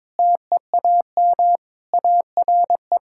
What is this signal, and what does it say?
Morse code, heard as a keyed tone, about 21 wpm, 700 Hz. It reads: TEAM ARE
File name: practice / 22wpm / Sentences with 3-Word Phrases - 22wpm